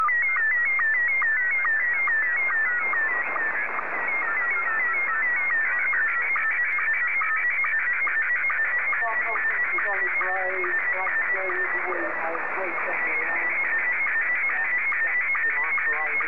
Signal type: SSTV (slow-scan television)

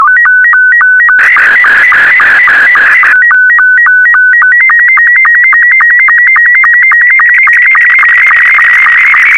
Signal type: SSTV (slow-scan television)

